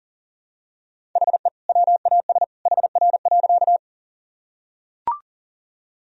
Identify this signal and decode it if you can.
Morse code, heard as a keyed tone, about 40 wpm, 700 Hz. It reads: HE WAS HR.
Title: Sentences from Top 200 Words and Abbreviations 40wpm